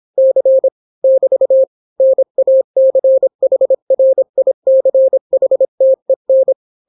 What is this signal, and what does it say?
Morse code, heard as a keyed tone, about 26 wpm, 530 Hz. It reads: C <BT> NACHRICHTEN